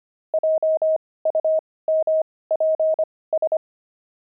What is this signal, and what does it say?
Morse code, heard as a keyed tone, about 25 wpm, spaced as for 18 wpm, 630 Hz. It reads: JUMPS
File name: pangram-fs-18-cs-25